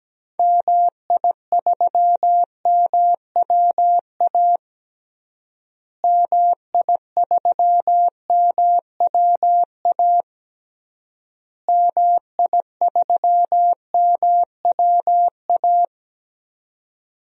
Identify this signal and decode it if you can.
Morse code, heard as a keyed tone, about 17 wpm, 700 Hz. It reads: MI3MWA MI3MWA MI3MWA
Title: DXCC-100 Call Signs - 6 Characters - Repeated 3x - 17wpm